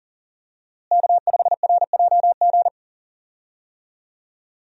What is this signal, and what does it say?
Morse code, heard as a keyed tone, about 40 wpm, 700 Hz. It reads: K5RJG
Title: USA Call Signs - 5 Characters - Repeated 3x - 40wpm